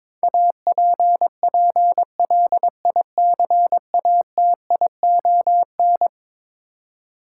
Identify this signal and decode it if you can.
Morse code, heard as a keyed tone, about 22 wpm, 700 Hz. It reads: APPLICATION
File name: practice / 22wpm / Eleven Letter Words - 22wpm